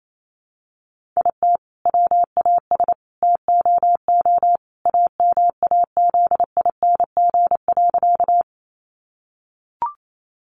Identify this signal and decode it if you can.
Morse code, heard as a keyed tone, about 28 wpm, 700 Hz. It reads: IT WAS TOO AMAZING.
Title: Sentences from Top 1500 Words 28wpm